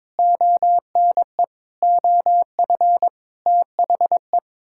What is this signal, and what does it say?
Morse code, heard as a keyed tone, about 22 wpm, 700 Hz. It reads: ONE OF THE
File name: Sets of 9 Words - 22wpm